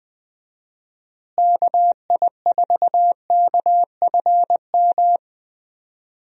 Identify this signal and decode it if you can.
Morse code, heard as a keyed tone, about 20 wpm, 700 Hz. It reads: KI4KFM
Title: USA Call Signs - 6 Characters - Repeated 3x - 20wpm